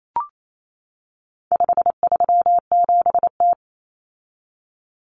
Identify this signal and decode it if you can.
Morse code, heard as a keyed tone, about 28 wpm, 700 Hz. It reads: 537T
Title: Common Cut Numbers - 4-Digit 28wpm